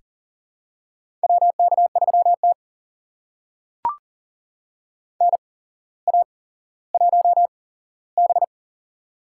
Morse code, heard as a keyed tone, about 41 wpm, spaced as for 17 wpm, 700 Hz. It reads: WK3T NA1B
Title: USA Call Signs - 4 Characters - 8x Character Spacing - 40wpm